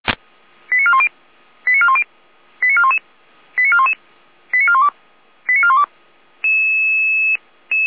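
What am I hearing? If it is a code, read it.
Five-tone selective call (ZVEI), heard as quick runs of five tones, tones of about 70 ms. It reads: 89410 89410 89411 89411 89412 89412
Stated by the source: ZVEI five-tone selective call received on an SDR.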